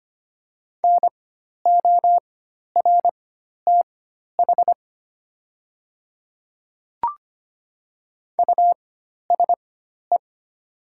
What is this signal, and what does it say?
Morse code, heard as a keyed tone, about 25 wpm, spaced as for 12 wpm, 700 Hz. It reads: NORTH USE
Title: 4x Character Spacing - Top 500 Words - 25wpm